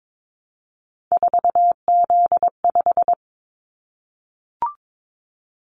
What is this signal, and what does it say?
Morse code, heard as a keyed tone, about 22 wpm, 700 Hz. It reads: 4Z5
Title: Callsign Prefixes for Top 50 DXCC - 22wpm